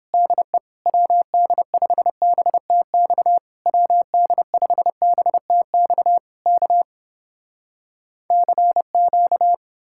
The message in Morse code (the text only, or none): DE WD5BTX WD5BTX K CQ